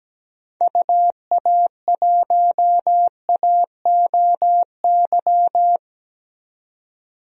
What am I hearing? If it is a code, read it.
Morse code, heard as a keyed tone, about 17 wpm, 700 Hz. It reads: UA1AOY
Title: DXCC-100 Call Signs - 6 Characters - 17wpm